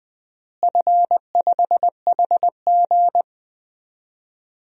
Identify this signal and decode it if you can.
Morse code, heard as a keyed tone, about 20 wpm, 700 Hz. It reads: F5HG
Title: DXCC-100 Call Signs - 4 Characters - Repeated 3x - 20wpm